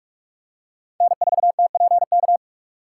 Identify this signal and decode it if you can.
Morse code, heard as a keyed tone, about 47 wpm, 700 Hz. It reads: N4TPK
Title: USA Call Signs - 5 Characters - 6x Character Spacing - 45wpm